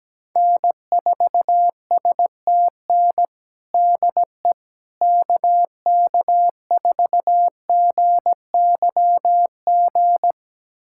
Morse code, heard as a keyed tone, about 17 wpm, 700 Hz. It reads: N4STN DE KK4GYG
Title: USA Callsign DE Callsign 17wpm